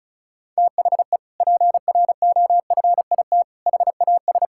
Morse code, heard as a keyed tone, about 34 wpm, 700 Hz. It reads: THE PROFIT HAS
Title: Sentences from Top 1000 Words - Review Part 5 - 35wpm